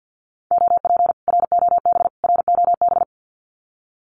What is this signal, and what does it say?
Morse code, heard as a keyed tone, about 48 wpm, 700 Hz. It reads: OP ROB ROB
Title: Common QSO Phrases Encore 50wpm